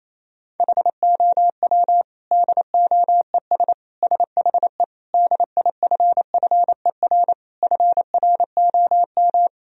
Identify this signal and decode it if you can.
Morse code, heard as a keyed tone, about 28 wpm, 700 Hz. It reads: HOW DOES SHE DIFFER FROM
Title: Sentences from Top 1000 Words - Review Part 1 - 28wpm